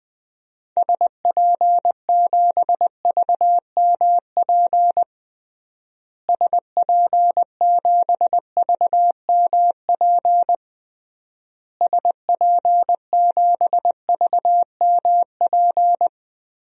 Morse code, heard as a keyed tone, about 20 wpm, 700 Hz. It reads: SP7VMP SP7VMP SP7VMP
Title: DXCC-100 Call Signs - 6 Characters - Repeated 3x - 20wpm